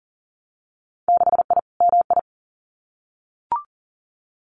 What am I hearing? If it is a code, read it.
Morse code, heard as a keyed tone, about 38 wpm, 700 Hz. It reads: 6I MI
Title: Winter Field Day Exchanges Encore 40wpm